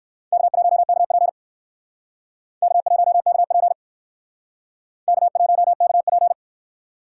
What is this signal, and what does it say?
Morse code, heard as a keyed tone, about 58 wpm, 700 Hz. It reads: K1KP K1KP K1KP